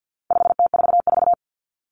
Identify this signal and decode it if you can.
Morse code, heard as a keyed tone, about 48 wpm, 700 Hz. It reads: 5T44
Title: Common Cut Numbers - 4-Digit 50wpm